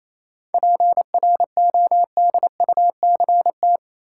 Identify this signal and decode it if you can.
Morse code, heard as a keyed tone, about 28 wpm, 700 Hz. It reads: PRODUCT
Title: Sentences from Top 1400 Words - New Word Drill Alt - 28wpm